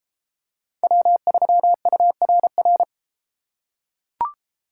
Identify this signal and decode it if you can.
Morse code, heard as a keyed tone, about 32 wpm, 700 Hz. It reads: W3URR